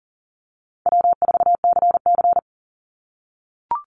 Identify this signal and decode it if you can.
Morse code, heard as a keyed tone, about 40 wpm, 700 Hz. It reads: W4CC
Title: Call Signs - 40wpm